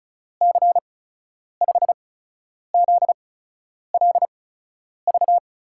Morse code, heard as a keyed tone, about 36 wpm, spaced as for 10 wpm, 700 Hz. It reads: C5ZLV
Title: USA Call Signs - 6 Characters - 8x Character Spacing - 35wpm